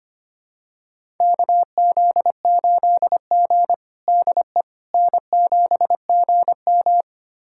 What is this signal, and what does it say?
Morse code, heard as a keyed tone, about 25 wpm, 700 Hz. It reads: KZ8G DE N7GM
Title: USA Callsign DE Callsign - 4 Characters - 25wpm